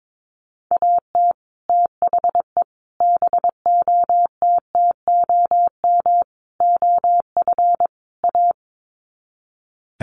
Morse code, heard as a keyed tone, about 22 wpm, 700 Hz. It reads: AT THE BOTTOM OF A
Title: Sets of 5 Words Encore2 22wpm